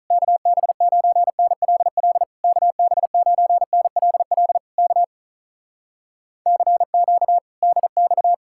Morse code, heard as a keyed tone, about 47 wpm, 700 Hz. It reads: KB9NLL KB9NLL K CQ DX